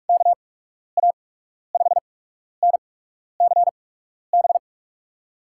Morse code, heard as a keyed tone, about 44 wpm, spaced as for 12 wpm, 700 Hz. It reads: KA5NCB
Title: USA Call Signs - 6 Characters - 8x Character Spacing - 45wpm